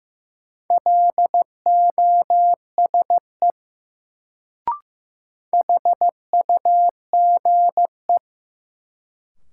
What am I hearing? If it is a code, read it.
Morse code, heard as a keyed tone, about 15 wpm, 700 Hz. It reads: LOSE HUGE
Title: Sentences from Top 1100 Words - New Word Drill Alt - 15wpm